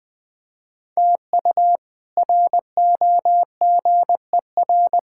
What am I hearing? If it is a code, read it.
Morse code, heard as a keyed tone, about 20 wpm, 700 Hz. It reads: TU ROGER
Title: K1USN SST QSO Practice - 20wpm